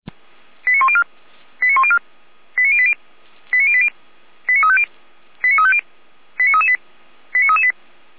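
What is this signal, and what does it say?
Five-tone selective call (ZVEI), heard as quick runs of five tones, tones of about 70 ms. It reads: 89104 89104 89080 89080 89350 89350 89338 89338
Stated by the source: ZVEI five-tone selective call 89358 received via SDR.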